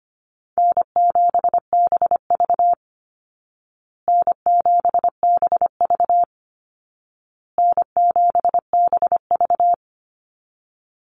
Morse code, heard as a keyed tone, about 25 wpm, 700 Hz. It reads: N7BV N7BV N7BV
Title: Easy Call Signs Repeated 3x - 25wpm